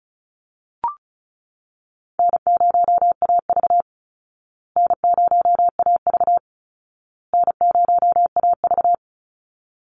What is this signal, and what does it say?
Morse code, heard as a keyed tone, about 34 wpm, 700 Hz. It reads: N0AV N0AV N0AV
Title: Easy Call Signs Repeated 3x - 35wpm